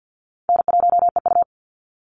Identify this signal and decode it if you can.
Morse code, heard as a keyed tone, about 50 wpm, 700 Hz. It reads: N1EU
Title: NA SOTA Chaser Callsigns 50wpm